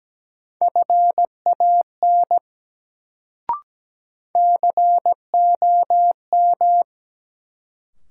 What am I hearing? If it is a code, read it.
Morse code, heard as a keyed tone, about 17 wpm, 700 Hz. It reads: FAN COM
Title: Sentences from Top 1300 Words - New Word Drill Alt - 17wpm